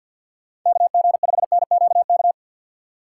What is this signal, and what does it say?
Morse code, heard as a keyed tone, about 48 wpm, 700 Hz. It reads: KG5NQK
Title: USA Call Signs - 6 Characters - 3x Character Spacing - 50wpm